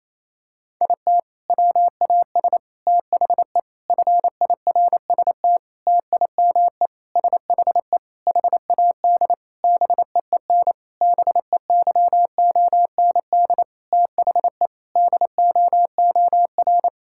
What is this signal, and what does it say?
Morse code, heard as a keyed tone, about 28 wpm, 700 Hz. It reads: IT WAS THE FIRST TIME SHE HAD BEEN BEYOND THE DOOR
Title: Sentences from Top 1000 Words - Review Part 6 - 28wpm